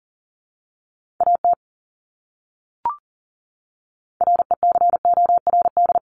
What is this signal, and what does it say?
Morse code, heard as a keyed tone, about 40 wpm, 700 Hz. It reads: AT RECORD
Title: Top 500 Words Repeated 40wpm